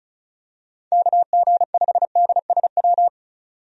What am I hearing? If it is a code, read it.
Morse code, heard as a keyed tone, about 36 wpm, 700 Hz. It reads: KG5DSW
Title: USA Call Signs - 6 Characters - 35wpm